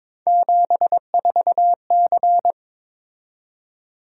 Morse code, heard as a keyed tone, about 22 wpm, 700 Hz. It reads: 74C
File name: practice / 22wpm / Three Letter-Number Combinations - 22wpm